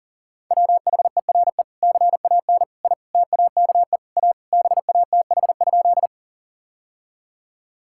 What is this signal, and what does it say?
Morse code, heard as a keyed tone, about 40 wpm, 700 Hz. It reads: WHERE CAN I TAKE A BATH?
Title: Sentences from Top 1600 Words 40wpm